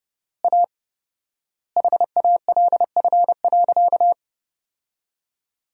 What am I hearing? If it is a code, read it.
Morse code, heard as a keyed tone, about 30 wpm, 700 Hz. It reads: A HALF.
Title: 4x Word-Spacing - Sentences from Top 400 Words - 30wpm